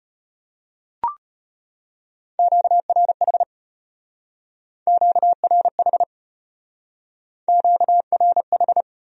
Morse code, heard as a keyed tone, about 33 wpm, 700 Hz. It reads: QRH QRH QRH